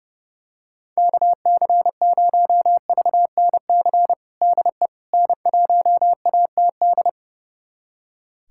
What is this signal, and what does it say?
Morse code, heard as a keyed tone, about 30 wpm, 700 Hz. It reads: KC0VNC DE N1ATD
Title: USA Callsign DE Callsign 30wpm